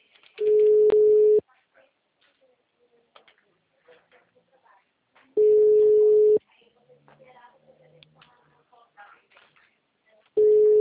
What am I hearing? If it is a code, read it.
Ringback tone (Europe, 425 Hz), with speech.